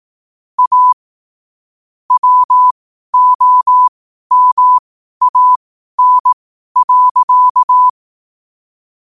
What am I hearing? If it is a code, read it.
Morse code, heard as a keyed tone, about 18 wpm, spaced as for 12 wpm, 1000 Hz. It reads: A WOMAN.
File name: qotdfiles / 12wpm